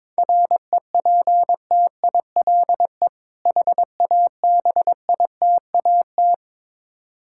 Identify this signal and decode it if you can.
Morse code, heard as a keyed tone, about 22 wpm, 690 Hz. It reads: REPTILE HABITAT
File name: Difficult Words with Hints - 22wpm